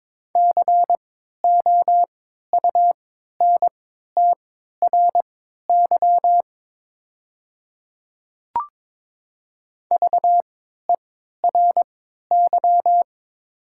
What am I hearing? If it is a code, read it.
Morse code, heard as a keyed tone, about 22 wpm, spaced as for 13 wpm, 690 Hz. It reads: COUNTRY VERY
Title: 3x Character Spacing - Top 300 Words - 22wpm